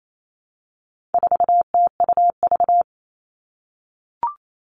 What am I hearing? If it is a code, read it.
Morse code, heard as a keyed tone, about 28 wpm, 700 Hz. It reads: 4TUV
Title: All Cut Numbers - 4-Digit 28wpm